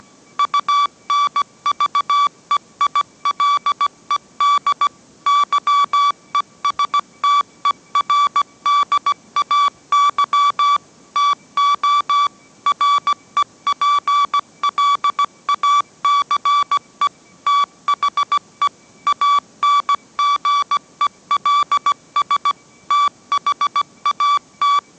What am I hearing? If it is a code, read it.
Morse code, heard as a keyed tone, about 17 wpm, 1150 Hz. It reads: UNVEILED YESTERDAY TO REPLACE THE ANGELS THAT
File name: mrx / mrx080 18wpm